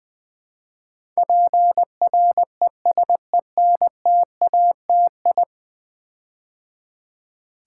Morse code, heard as a keyed tone, about 20 wpm, 700 Hz. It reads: PRESENTATI
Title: Build A Word - 5 Letters At A Time - 12-Letter Words - 20wpm